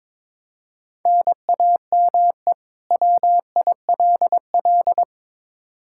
Morse code, heard as a keyed tone, about 22 wpm, 700 Hz. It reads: NAME WILL
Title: K1USN SST QSO Practice 22wpm